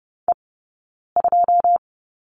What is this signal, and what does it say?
Morse code, heard as a keyed tone, about 30 wpm, spaced as for 9 wpm, 690 Hz. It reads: E2
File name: course / 30:10wpm / 086 Lesson 15d - Number 5 - Two Characters 30wpm